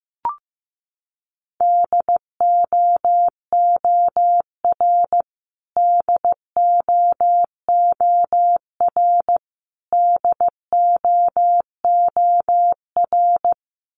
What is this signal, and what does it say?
Morse code, heard as a keyed tone, about 15 wpm, 700 Hz. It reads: DOOR DOOR DOOR
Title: Top 300 Words Repeated 15wpm